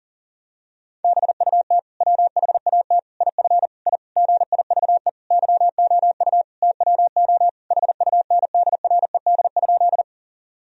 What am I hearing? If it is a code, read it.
Morse code, heard as a keyed tone, about 40 wpm, 700 Hz. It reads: BUT WHAT IF I GIVE YOU TWO HUNDRED?
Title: Sentences from Top 1000 Words - Review Part 2 - 40wpm